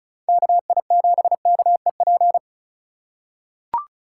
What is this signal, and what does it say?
Morse code, heard as a keyed tone, about 36 wpm, 700 Hz. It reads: KI7KEP
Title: USA Call Signs - 6 Characters - 3x Character Spacing - 35wpm